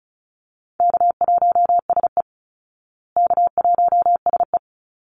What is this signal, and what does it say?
Morse code, heard as a keyed tone, about 34 wpm, 700 Hz. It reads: K1SE K1SE
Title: Easy Call Signs Repeated 3x - 35wpm